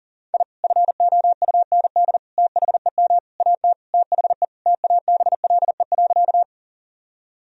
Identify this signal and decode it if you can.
Morse code, heard as a keyed tone, about 40 wpm, 700 Hz. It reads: I FOUND THEM AT THE TABLE.
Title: Sentences from Top 1000 Words - Review Part 2 - 40wpm